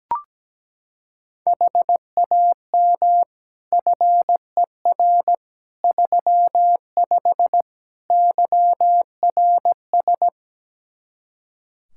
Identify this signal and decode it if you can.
Morse code, heard as a keyed tone, about 17 wpm, 700 Hz. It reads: HAM FER 35 YRS
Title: Practice QSO - Exchange 3 - More Info - 17wpm